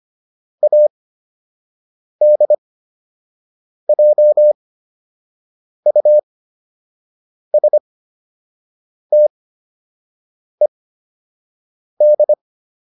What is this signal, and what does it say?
Morse code, heard as a keyed tone, about 25 wpm, spaced as for 6 wpm, 600 Hz. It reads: ADJUSTED
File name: ED8 06